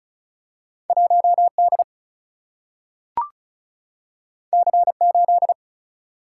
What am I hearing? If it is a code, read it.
Morse code, heard as a keyed tone, about 34 wpm, 700 Hz. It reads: 1D C8
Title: Single-Letter Single-Number 35wpm